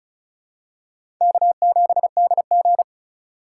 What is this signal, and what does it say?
Morse code, heard as a keyed tone, about 36 wpm, 700 Hz. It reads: K7DG
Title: USA Call Signs - 4 Characters - 2x Character Spacing - 35wpm